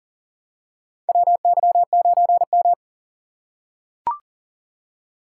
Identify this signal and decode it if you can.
Morse code, heard as a keyed tone, about 40 wpm, 700 Hz. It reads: WY9M
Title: USA Call Signs - 4 Characters - 6x Character Spacing - 40wpm